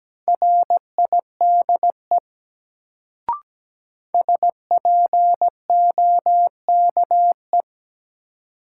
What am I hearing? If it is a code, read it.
Morse code, heard as a keyed tone, about 17 wpm, 700 Hz. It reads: RIDE SPOKE
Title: Sentences from Top 1200 Words - New Word Drill Alt - 17wpm